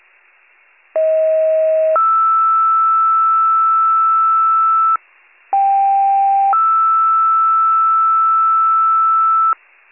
Two-tone page, heard as a one-second tone, then a three-second tone, about 640 then 1320 Hz, about 790 then 1320 Hz. It is heard from 1.0 to 9.5 s.